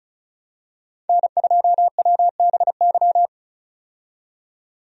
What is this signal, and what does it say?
Morse code, heard as a keyed tone, about 34 wpm, 700 Hz. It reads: N2WBY
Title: USA Call Signs - 5 Characters - 35wpm